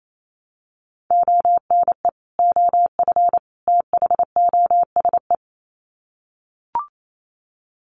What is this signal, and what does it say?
Morse code, heard as a keyed tone, about 28 wpm, 700 Hz. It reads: ONE OF THOSE
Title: Sets of 3 Words 28wpm